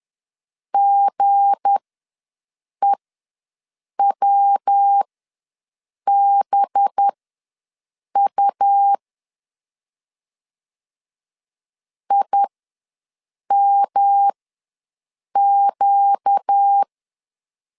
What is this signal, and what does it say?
Morse code, heard as a keyed tone, about 10 wpm, spaced as for 6 wpm, 800 Hz. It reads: GEWBU IMQ
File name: LEZIONI / 7 GF9Z / 7 G ws 10wpm 3843